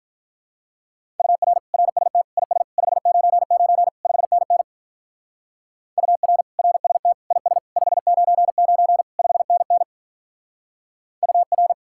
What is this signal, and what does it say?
Morse code, heard as a keyed tone, about 45 wpm, 700 Hz. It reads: UR RST IS 599 5NN UR RST IS 599 5NN UR